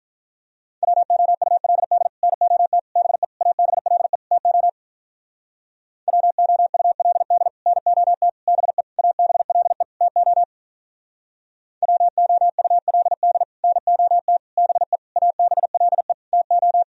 Morse code, heard as a keyed tone, about 46 wpm, 700 Hz. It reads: WOULD NOT BE ABLE TO WOULD NOT BE ABLE TO WOULD NOT BE ABLE TO